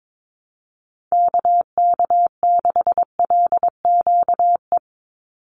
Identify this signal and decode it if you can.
Morse code, heard as a keyed tone, about 22 wpm, 700 Hz. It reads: KK6LQE
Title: USA Call Signs 22wpm